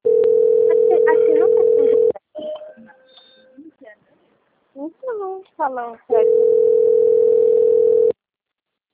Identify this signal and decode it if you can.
Ringback tone (North America), 440 + 480 Hz, with speech.
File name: out- - - - - .57670